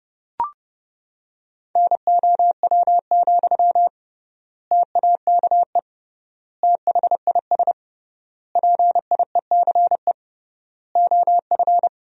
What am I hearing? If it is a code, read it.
Morse code, heard as a keyed tone, about 30 wpm, 700 Hz. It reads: NOW, TAKE THIS PIECE OF
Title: 3x Word-Spacing - Sentences from Top 400 Words - 30wpm